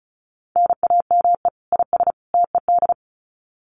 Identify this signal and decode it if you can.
Morse code, heard as a keyed tone, about 34 wpm, 700 Hz. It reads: NAME IS TED
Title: Common QSO Phrases 35wpm